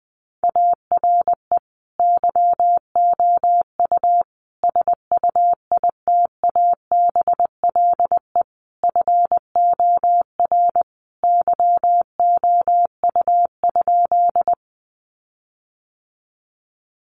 Morse code, heard as a keyed tone, about 20 wpm, 700 Hz. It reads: ARE YOU SUITABLE FOR YOU?
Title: Sentences from Top 1400 Words 20wpm